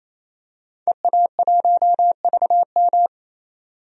Morse code, heard as a keyed tone, about 28 wpm, 700 Hz. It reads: EA1VM
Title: DXCC-100 Call Signs - 5 Characters - 28wpm